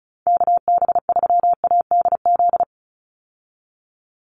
Morse code, heard as a keyed tone, about 36 wpm, 700 Hz. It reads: KB3ADZ
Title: USA Call Signs 35wpm